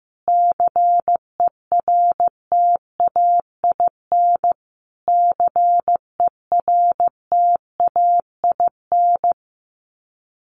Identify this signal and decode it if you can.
Morse code, heard as a keyed tone, about 15 wpm, 700 Hz. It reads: CERTAIN CERTAIN
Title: Top 500 Words Repeated 15wpm